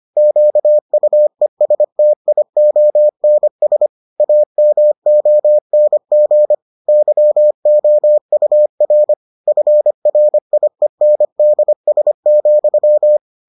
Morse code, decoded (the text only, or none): QUESTIONS AMONG YOUR FRIENDS,